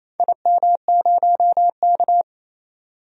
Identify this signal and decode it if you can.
Morse code, heard as a keyed tone, about 28 wpm, 700 Hz. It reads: IM0K
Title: DXCC-100 Call Signs - 4 Characters - 28wpm